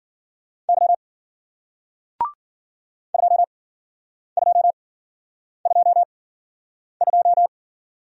Morse code, heard as a keyed tone, about 55 wpm, spaced as for 9 wpm, 700 Hz. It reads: Y 2222